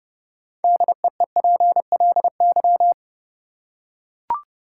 Morse code, heard as a keyed tone, about 29 wpm, 700 Hz. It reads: DEEPLY
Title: Six Letter Words - 30wpm